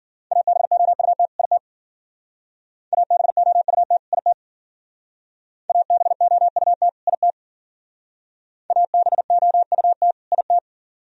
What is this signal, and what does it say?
Morse code, heard as a keyed tone, about 53 wpm, 700 Hz. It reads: ABOUT IT ABOUT IT ABOUT IT ABOUT IT